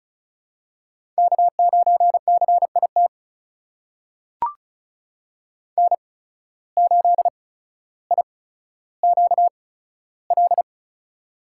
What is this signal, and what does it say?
Morse code, heard as a keyed tone, about 36 wpm, spaced as for 15 wpm, 700 Hz. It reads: K9CIT N8IQL
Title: USA Call Signs - 5 Characters - 8x Character Spacing - 35wpm